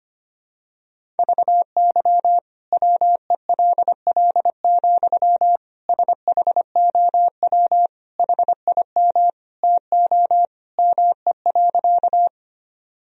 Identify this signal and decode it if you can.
Morse code, heard as a keyed tone, about 25 wpm, 700 Hz. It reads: VY WELL, SHOW HIM TO ME.
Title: Sentences from Top 200 Words and Abbreviations 25wpm